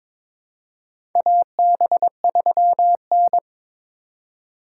Morse code, heard as a keyed tone, about 22 wpm, 700 Hz. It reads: AB3N
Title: USA Call Signs - 4 Characters - 4x Character Spacing - 22wpm